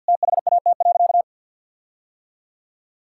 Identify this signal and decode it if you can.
Morse code, heard as a keyed tone, about 50 wpm, 700 Hz. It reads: THAT.
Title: Sentences from Top 1600 Words 50wpm